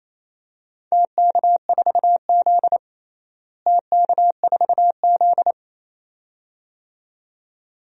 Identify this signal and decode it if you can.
Morse code, heard as a keyed tone, about 28 wpm, 700 Hz. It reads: TK4Z TK4Z
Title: DXCC-100 Call Signs - 4 Characters - Repeated 2x - 28wpm